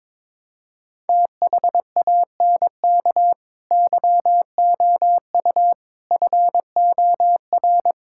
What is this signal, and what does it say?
Morse code, heard as a keyed tone, about 22 wpm, 700 Hz. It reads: THANK YOU FOR
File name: practice / 22wpm / Sets of 7 Words - 22wpm